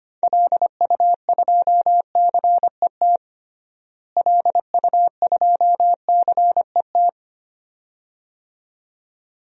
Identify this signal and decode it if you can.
Morse code, heard as a keyed tone, about 25 wpm, 700 Hz. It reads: LU2CET LU2CET
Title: DXCC-100 Call Signs - 6 Characters - Repeated 2x - 25wpm